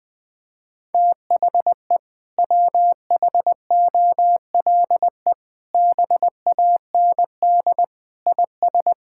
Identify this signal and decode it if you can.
Morse code, heard as a keyed tone, about 20 wpm, 700 Hz. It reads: THE WHOLE BAND IS
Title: Sentences from Top 1600 Words 20wpm